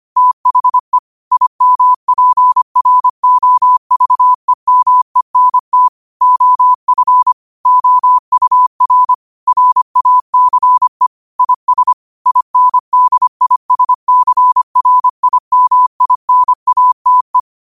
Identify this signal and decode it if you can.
Morse code, heard as a keyed tone, about 25 wpm, 1000 Hz. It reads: THE IMPROVEMENT OF OUR RACE IS INDISCRIMINATE
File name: qotdfiles / 25wpm